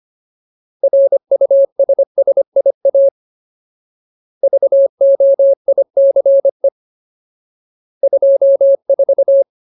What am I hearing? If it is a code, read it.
Morse code, heard as a keyed tone, about 25 wpm, 550 Hz. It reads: RUSSIA VOICE 24
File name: Ultimate Word List / 25WPM Ultimate